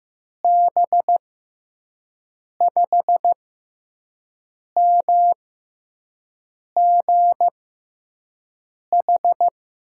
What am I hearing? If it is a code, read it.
Morse code, heard as a keyed tone, about 15 wpm, spaced as for 5 wpm, 700 Hz. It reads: B5MGH
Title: USA Call Signs - 6 Characters - 6x Character Spacing - 15wpm